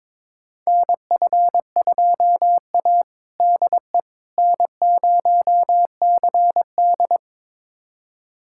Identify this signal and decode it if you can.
Morse code, heard as a keyed tone, about 22 wpm, 700 Hz. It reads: NF2A DE N0CD
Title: USA Callsign DE Callsign - 4 Characters - 22wpm